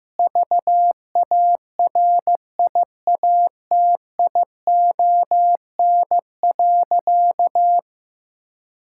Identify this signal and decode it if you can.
Morse code, heard as a keyed tone, about 15 wpm, 700 Hz. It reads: VARIATION.
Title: Sentences from Top 1600 Words 15wpm